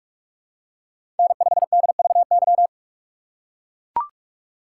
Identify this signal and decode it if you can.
Morse code, heard as a keyed tone, about 44 wpm, 700 Hz. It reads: N5DVY
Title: USA Call Signs - 5 Characters - 3x Character Spacing - 45wpm